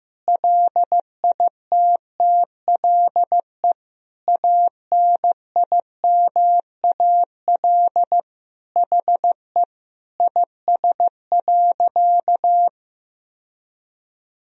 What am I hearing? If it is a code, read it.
Morse code, heard as a keyed tone, about 15 wpm, 700 Hz. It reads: LITTLE ANIMAL HE IS.
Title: Sentences from Top 1000 Words - Review Part 1 - 15wpm